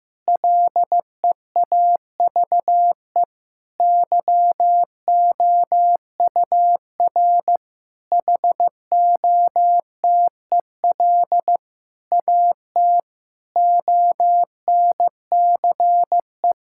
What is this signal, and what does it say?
Morse code, heard as a keyed tone, about 15 wpm, 700 Hz. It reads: LEAVE YOUR HOTEL AT ONCE
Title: Sentences from Top 1600 Words 15wpm